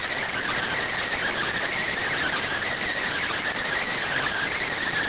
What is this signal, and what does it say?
Emergency Alert System (SAME) tones (header bursts), heard as harsh buzzing bursts. It is heard from 0.2 to 4.7 s.